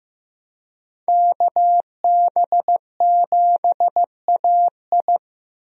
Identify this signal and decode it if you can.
Morse code, heard as a keyed tone, about 15 wpm, 700 Hz. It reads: KB7AI
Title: USA Call Signs - 5 Characters - Repeated 3x - 15wpm